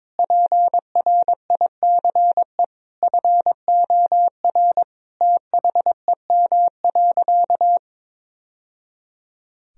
Morse code, heard as a keyed tone, about 22 wpm, 700 Hz. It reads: PRICE FOR THEM.
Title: Sentences from Top 1000 Words - Review Part 4 - 22wpm